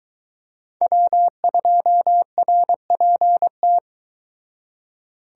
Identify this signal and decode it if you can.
Morse code, heard as a keyed tone, about 22 wpm, 700 Hz. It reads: W2RPT